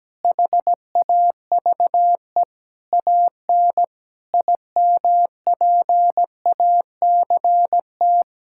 Morse code, heard as a keyed tone, about 17 wpm, 700 Hz. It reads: HAVE AN IMPACT
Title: Sentences from Top 1000 Words - Review Part 6 - 17wpm